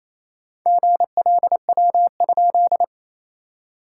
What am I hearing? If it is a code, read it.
Morse code, heard as a keyed tone, about 27 wpm, 660 Hz. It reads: GLW?
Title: SOTA QSO Practice 28wpm